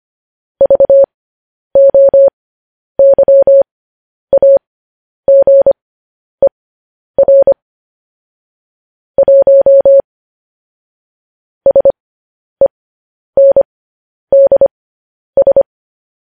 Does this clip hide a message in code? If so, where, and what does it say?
Yes: Morse code from 0.6 to 15.6 s, heard as a keyed tone, about 25 wpm, spaced as for 10 wpm, 550 Hz. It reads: VOYAGER 1 SENDS